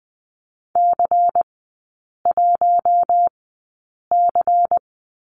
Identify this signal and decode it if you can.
Morse code, heard as a keyed tone, about 20 wpm, spaced as for 8 wpm, 700 Hz. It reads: C1C
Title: 100 Lesson 17e - Letter u - Three Characters 20wpm